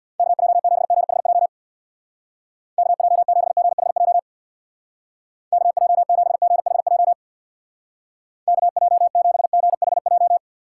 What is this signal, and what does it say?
Morse code, heard as a keyed tone, about 62 wpm, 700 Hz. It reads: KJ7GHJ KJ7GHJ KJ7GHJ KJ7GHJ
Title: Speed Racing - USA 2x3 Call Signs 50wpm